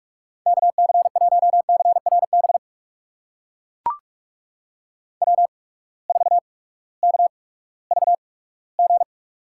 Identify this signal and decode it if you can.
Morse code, heard as a keyed tone, about 44 wpm, 690 Hz. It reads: K C 1 C R B W 4 K V G
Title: USA Call Signs - 8x Character Spacing - 45wpm